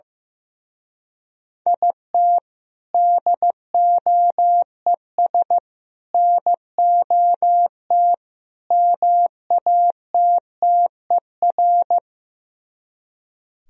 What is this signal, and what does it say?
Morse code, heard as a keyed tone, about 15 wpm, 700 Hz. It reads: IT DOES NOT MATTER
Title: Speed-Racing - Sets of 4 Words 15wpm